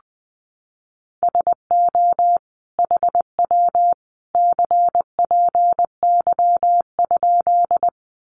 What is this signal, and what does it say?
Morse code, heard as a keyed tone, about 20 wpm, 700 Hz. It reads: SO HW CPY?
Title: Common QSO Phrases 20wpm